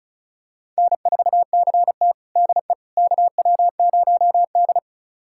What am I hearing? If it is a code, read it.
Morse code, heard as a keyed tone, about 36 wpm, 700 Hz. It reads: N4CT DE KW0D
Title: USA Callsign DE Callsign - 4 Characters - 35wpm